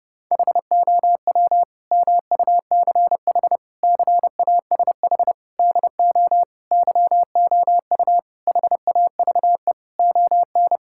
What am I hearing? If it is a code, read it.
Morse code, heard as a keyed tone, about 32 wpm, 700 Hz. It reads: HOW MUCH CASH DO YOU HAVE ON
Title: Sentences from Top 1000 Words - Review Part 5 - 30wpm